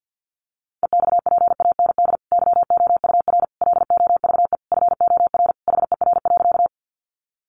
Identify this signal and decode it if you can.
Morse code, heard as a keyed tone, about 50 wpm, 700 Hz. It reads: EXPAND YOUR LOVE FOR HER.
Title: Sentences from Top 1400 Words 50wpm